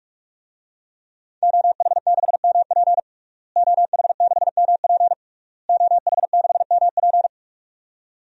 Morse code, heard as a keyed tone, about 46 wpm, 700 Hz. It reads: OH6MP OH6MP OH6MP
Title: DXCC-100 Call Signs - 5 Characters - Repeated 3x - 45wpm